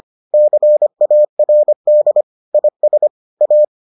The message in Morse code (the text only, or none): CARD IS A